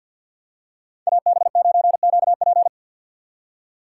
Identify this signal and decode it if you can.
Morse code, heard as a keyed tone, about 48 wpm, 700 Hz. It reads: AB9QP
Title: USA Call Signs - 5 Characters - 2x Character Spacing - 50wpm